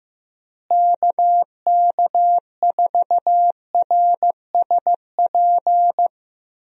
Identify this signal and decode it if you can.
Morse code, heard as a keyed tone, about 16 wpm, 700 Hz. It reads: KK4RSP
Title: USA Call Signs - 6 Characters - Repeated 2x - 15wpm